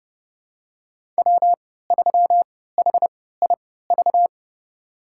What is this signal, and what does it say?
Morse code, heard as a keyed tone, about 30 wpm, spaced as for 17 wpm, 700 Hz. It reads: W3HIV
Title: USA Call Signs - 5 Characters - 3x Character Spacing - 30wpm